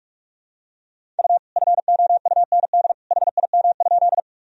Morse code, heard as a keyed tone, about 46 wpm, 700 Hz. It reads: U FOUND HIM?
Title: Sentences from Top 300 Words and Abbreviations 45wpm